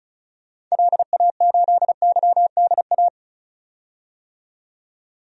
Morse code, heard as a keyed tone, about 36 wpm, 700 Hz. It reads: LA8YDA
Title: DXCC-100 Call Signs - 6 Characters - 35wpm